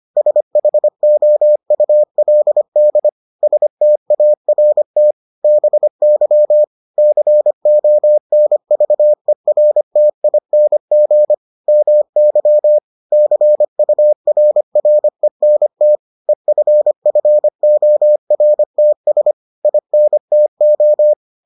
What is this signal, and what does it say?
Morse code, heard as a keyed tone, about 25 wpm, 600 Hz. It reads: SHOULD START BY CONVERTING MY CURRENT EFFORTS INTO